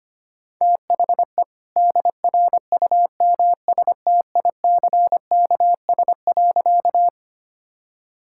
Morse code, heard as a keyed tone, about 25 wpm, 700 Hz. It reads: THE DRUMSTICKS.
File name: practice / 25wpm / All-Age Appropriate Jokes - 25wpm